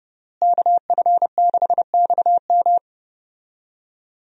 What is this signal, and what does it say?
Morse code, heard as a keyed tone, about 30 wpm, 700 Hz. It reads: KF6XM
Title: USA Call Signs - 5 Characters - Repeated 3x - 30wpm